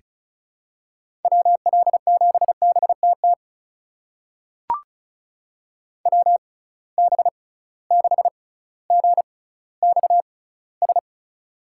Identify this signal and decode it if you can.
Morse code, heard as a keyed tone, about 36 wpm, spaced as for 18 wpm, 700 Hz. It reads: WL7BTT WB6GXS
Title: USA Call Signs - 6 Characters - 6x Character Spacing - 35wpm